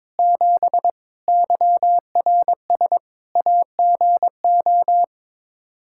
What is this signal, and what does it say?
Morse code, heard as a keyed tone, about 22 wpm, 700 Hz. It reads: 7 YRS AGO
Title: K1USN SST QSO Practice 22wpm